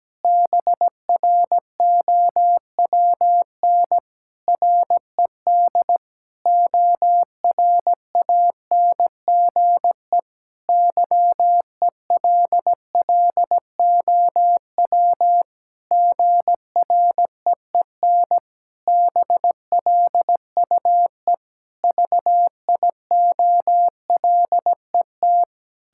Morse code, heard as a keyed tone, about 17 wpm, 700 Hz. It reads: BROWN RED ORANGE YELLOW GREEN BLUE VIOLET
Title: Pangram Challenge v2 - 17wpm